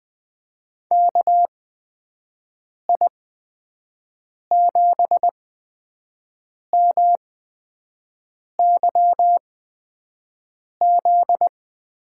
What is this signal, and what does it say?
Morse code, heard as a keyed tone, about 20 wpm, spaced as for 5 wpm, 700 Hz. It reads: KI7MYZ
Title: USA Call Signs - 6 Characters - 8x Character Spacing - 20wpm